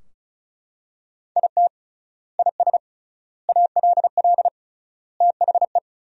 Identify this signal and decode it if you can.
Morse code, heard as a keyed tone, about 36 wpm, 700 Hz. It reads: IT IS ALL THE
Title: 3x Word-Spacing - Sentences from Top 200 Words - 35wpm